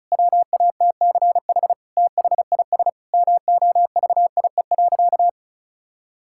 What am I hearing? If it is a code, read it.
Morse code, heard as a keyed tone, about 34 wpm, 700 Hz. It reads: WATCH THIS MOVIE.
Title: Sentences from Top 1000 Words - Review Part 5 - 35wpm